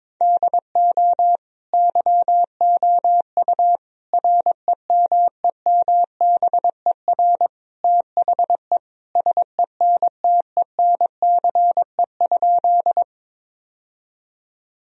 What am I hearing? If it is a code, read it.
Morse code, heard as a keyed tone, about 22 wpm, 700 Hz. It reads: DO YOU REMEMBER THE SENTENCE?
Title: Sentences from Top 1000 Words - Review Part 1 - 22wpm